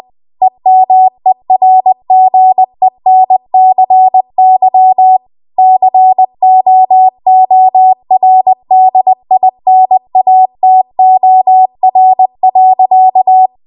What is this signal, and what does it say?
Morse code, heard as a keyed tone, about 20 wpm, 750 Hz. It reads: EMERGENCY COORDINATOR.
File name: Archive / 20 WPM / 20WPM